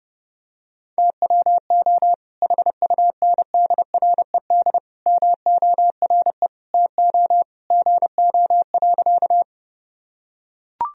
Morse code, heard as a keyed tone, about 30 wpm, 700 Hz. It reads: TWO HUNDRED MORE TO GO.
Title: Sentences from Top 1000 Words - Review Part 2 - 30wpm